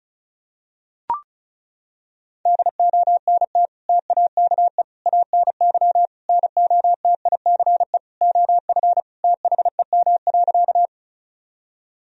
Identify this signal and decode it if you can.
Morse code, heard as a keyed tone, about 34 wpm, 700 Hz. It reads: DONT TAKE ANY NOTICE OF THEM.
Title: Sentences from Top 1000 Words - Review Part 2 - 35wpm